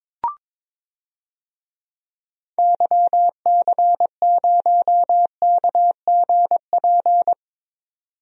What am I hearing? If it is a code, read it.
Morse code, heard as a keyed tone, about 21 wpm, 700 Hz. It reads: YC0KGP
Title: DXCC-100 Call Signs - 6 Characters - 22wpm